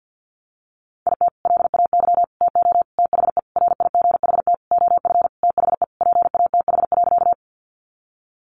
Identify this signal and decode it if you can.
Morse code, heard as a keyed tone, about 50 wpm, 700 Hz. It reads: IT LAY TO THE RIGHT OF THE PATH.
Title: Sentences from Top 1400 Words 50wpm